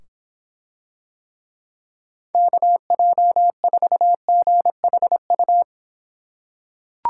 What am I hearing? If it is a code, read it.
Morse code, heard as a keyed tone, about 26 wpm, 700 Hz. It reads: KJ4GHU